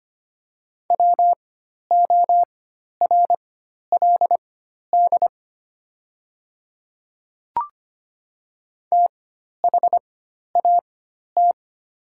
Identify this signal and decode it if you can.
Morse code, heard as a keyed tone, about 25 wpm, spaced as for 12 wpm, 700 Hz. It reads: WORLD THAT
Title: 4x Character Spacing - Top 200 Words - 25wpm